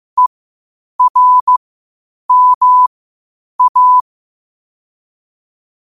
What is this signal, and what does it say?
Morse code, heard as a keyed tone, about 15 wpm, spaced as for 8 wpm, 1000 Hz. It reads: ERMA
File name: qotdfiles / 8wpm